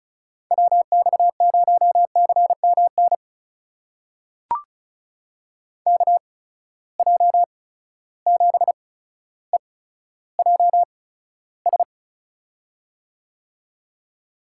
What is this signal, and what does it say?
Morse code, heard as a keyed tone, about 36 wpm, spaced as for 15 wpm, 700 Hz. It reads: WX0CMN KJ7EJS
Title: USA Call Signs - 6 Characters - 8x Character Spacing - 35wpm